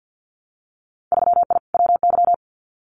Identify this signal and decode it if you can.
Morse code, heard as a keyed tone, about 50 wpm, 700 Hz. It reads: 3I WY
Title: Winter Field Day Exchanges Encore 50wpm